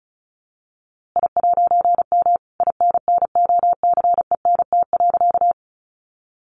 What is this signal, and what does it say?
Morse code, heard as a keyed tone, about 34 wpm, 700 Hz. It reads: I'M INNOCENT.
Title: A Sound Of Thunder 35wpm